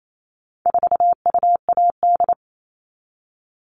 Morse code, heard as a keyed tone, about 28 wpm, 700 Hz. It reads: 4UAD
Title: All Cut Numbers - 4-Digit 28wpm